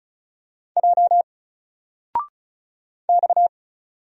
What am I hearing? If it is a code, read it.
Morse code, heard as a keyed tone, about 36 wpm, 700 Hz. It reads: J X
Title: Single Character (Letter:Number:Punctuation:Prosign) Alt 35wpm